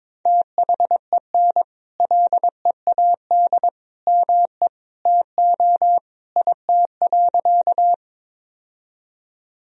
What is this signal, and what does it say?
Morse code, heard as a keyed tone, about 22 wpm, 700 Hz. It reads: THEN LEAD ME TO IT.
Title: Sentences from Top 1000 Words - Review Part 2 - 22wpm